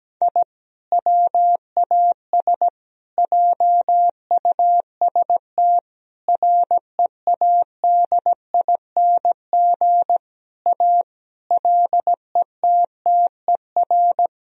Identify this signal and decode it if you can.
Morse code, heard as a keyed tone, about 17 wpm, 700 Hz. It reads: I WAS JUST READING A LETTER
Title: Sentences from Top 1000 Words - Review Part 3 - 17wpm